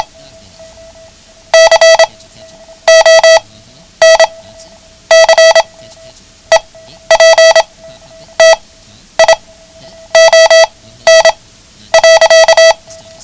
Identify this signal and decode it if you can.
Morse code, heard as a keyed tone, about 27 wpm, spaced as for 10 wpm, 650 Hz. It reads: CONCEPTION.
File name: ham radio / CodeQuick / Code Quick 09